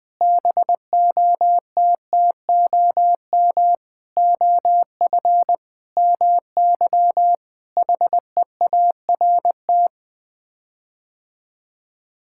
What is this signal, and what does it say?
Morse code, heard as a keyed tone, about 20 wpm, 700 Hz. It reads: BOTTOM OF MY HEART
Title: Sets of 8 Words - 20wpm